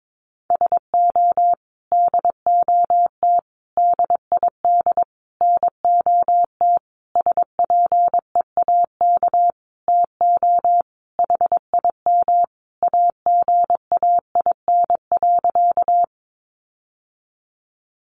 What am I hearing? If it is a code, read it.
Morse code, heard as a keyed tone, about 22 wpm, 700 Hz. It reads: SO DOT DID NOT SPEAK TO HIM AGAIN.
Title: Sentences from Top 1500 Words 22wpm